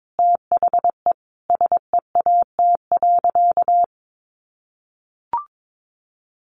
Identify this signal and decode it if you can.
Morse code, heard as a keyed tone, about 22 wpm, 700 Hz. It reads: THE SEAT.
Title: Sentences from Top 1500 Words 22wpm